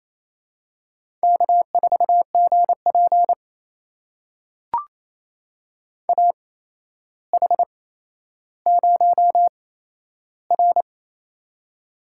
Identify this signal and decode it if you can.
Morse code, heard as a keyed tone, about 27 wpm, spaced as for 12 wpm, 700 Hz. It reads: K4GP AH0R
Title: USA Call Signs - 8x Character Spacing - 28wpm